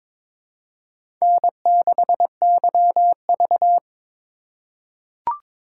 Morse code, heard as a keyed tone, about 22 wpm, 700 Hz. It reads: N6YV
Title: USA Call Signs - 4 Characters - 3x Character Spacing - 22wpm